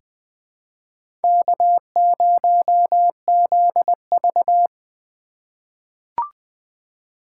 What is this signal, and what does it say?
Morse code, heard as a keyed tone, about 20 wpm, 700 Hz. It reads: K0ZV
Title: USA Call Signs - 4 Characters - 6x Character Spacing - 20wpm